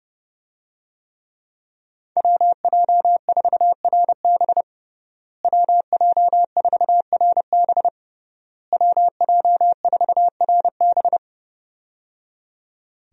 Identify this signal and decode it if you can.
Morse code, heard as a keyed tone, about 30 wpm, 700 Hz. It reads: WJ4RB WJ4RB WJ4RB
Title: USA Call Signs - 5 Characters - Repeated 3x - 30wpm